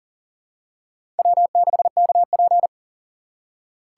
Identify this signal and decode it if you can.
Morse code, heard as a keyed tone, about 40 wpm, 700 Hz. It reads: W6KP
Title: USA Call Signs - 4 Characters - 2x Character Spacing - 40wpm